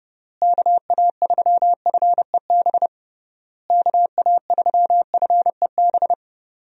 Morse code, heard as a keyed tone, about 29 wpm, 700 Hz. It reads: KA3FEB KA3FEB
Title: USA Call Signs - 6 Characters - Repeated 3x - 30wpm